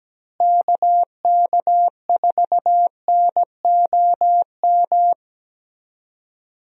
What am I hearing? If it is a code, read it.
Morse code, heard as a keyed tone, about 17 wpm, 700 Hz. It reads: KK4NOM
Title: USA Call Signs - 6 Characters - Repeated 3x - 17wpm